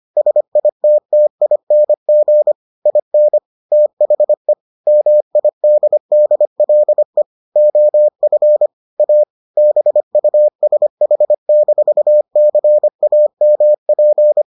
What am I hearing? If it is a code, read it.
Morse code, heard as a keyed tone, about 25 wpm, 600 Hz. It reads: SITTING IN THE MIDDLE OF A BUSH-CAMP